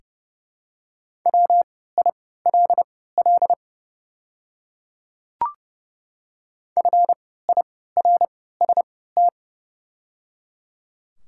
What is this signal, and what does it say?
Morse code, heard as a keyed tone, about 30 wpm, spaced as for 17 wpm, 700 Hz. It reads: WILL FIRST
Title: 3x Character Spacing - Top 100 Words - 30wpm